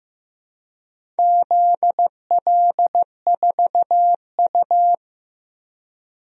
Morse code, heard as a keyed tone, about 15 wpm, 700 Hz. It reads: ZL4U
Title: DXCC-100 Call Signs - 4 Characters - 15wpm